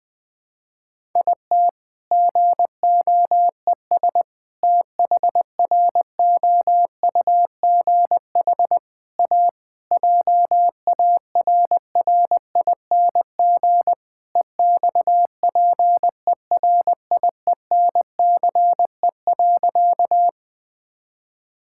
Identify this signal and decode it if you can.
Morse code, heard as a keyed tone, about 20 wpm, 700 Hz. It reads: IT GOES THROUGH A JARRING EXPERIENCE.
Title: All-Age Appropriate Jokes - 20wpm